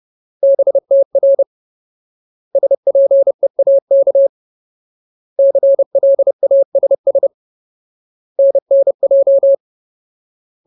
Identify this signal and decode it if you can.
Morse code, heard as a keyed tone, about 30 wpm, 550 Hz. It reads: BTR SPEAK CLASS NNJ
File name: Ultimate Word List / 30WPM Ultimate